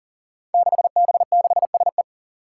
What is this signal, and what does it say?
Morse code, heard as a keyed tone, about 40 wpm, 700 Hz. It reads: 6B6SE
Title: Five Challenging Letter-Number Combinations 40wpm